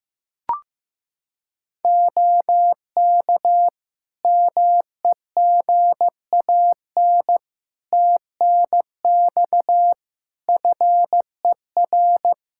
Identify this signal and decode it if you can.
Morse code, heard as a keyed tone, about 15 wpm, 700 Hz. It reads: OK MEGAN TNX FER
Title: Practice QSO - Exchange 4 - Ending - 15wpm